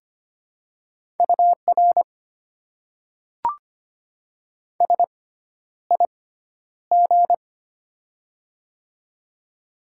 Morse code, heard as a keyed tone, about 25 wpm, spaced as for 11 wpm, 700 Hz. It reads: UR SIG
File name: Top 90 QSO Elements - 6x Character Spacing - 25wpm